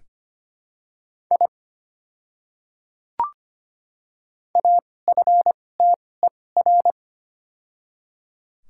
Morse code, heard as a keyed tone, about 25 wpm, spaced as for 18 wpm, 690 Hz. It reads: I AFTER
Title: 2x Character Spacing - Top 300 Words - 25wpm